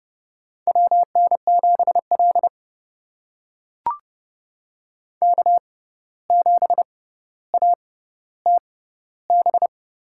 Morse code, heard as a keyed tone, about 31 wpm, spaced as for 15 wpm, 700 Hz. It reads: WN7L K7ATB
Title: USA Call Signs - 6x Character Spacing - 30wpm